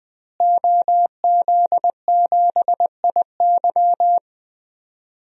Morse code, heard as a keyed tone, about 20 wpm, 700 Hz. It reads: OZ7IY
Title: DXCC-100 Call Signs - 5 Characters - Repeated 3x - 20wpm